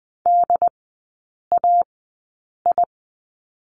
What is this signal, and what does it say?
Morse code, heard as a keyed tone, about 20 wpm, spaced as for 8 wpm, 690 Hz. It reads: DAI
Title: 081 Lesson 14e - Number 2 - Three Characters 20wpm